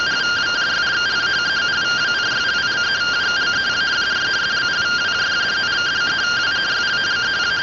Two-tone teleprinter signal, about 170 Hz shift.